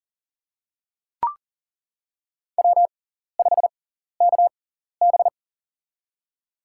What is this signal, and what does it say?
Morse code, heard as a keyed tone, about 40 wpm, spaced as for 14 wpm, 700 Hz. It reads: W5KB
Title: USA Call Signs - 4 Characters - 6x Character Spacing - 40wpm